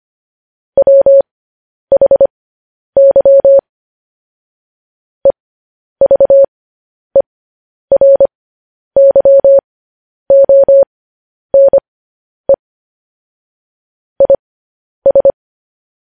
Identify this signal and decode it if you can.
Morse code, heard as a keyed tone, about 25 wpm, spaced as for 10 wpm, 550 Hz. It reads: WHY EVERYONE IS
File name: podcast / News headlines-1x-25x10